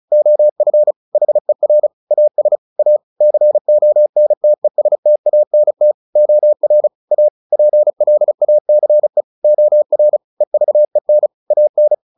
Morse code, heard as a keyed tone, about 34 wpm, 600 Hz. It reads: OF HER AS A CONTESTANT OR A PLACE OR EVEN AN
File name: the old man and the sea / the old man and the sea 1 35wpm